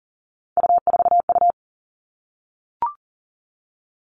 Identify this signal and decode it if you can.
Morse code, heard as a keyed tone, about 42 wpm, 700 Hz. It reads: U4U
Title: Three Challenging Letter-Number Combinations 40wpm